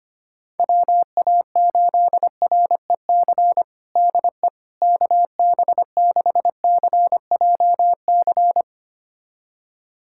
Morse code, heard as a keyed tone, about 25 wpm, 700 Hz. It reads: WA8REC DE KB6CJC
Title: USA Callsign DE Callsign - 6 Characters - 25wpm